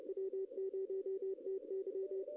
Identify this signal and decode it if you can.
Morse code, heard as a keyed tone, about 34 wpm, 410 Hz. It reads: W0TY
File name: contests / arrl-160-k5tr-op-w5tn -0200